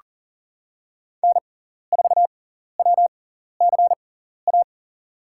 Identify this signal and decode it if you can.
Morse code, heard as a keyed tone, about 39 wpm, spaced as for 14 wpm, 700 Hz. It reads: N4WCA
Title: USA Call Signs - 5 Characters - 6x Character Spacing - 40wpm